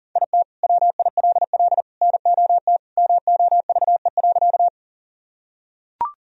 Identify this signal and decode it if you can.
Morse code, heard as a keyed tone, about 40 wpm, 700 Hz. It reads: IT WILL NOT MOVE.
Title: Sentences from Top 1000 Words - Review Part 1 - 40wpm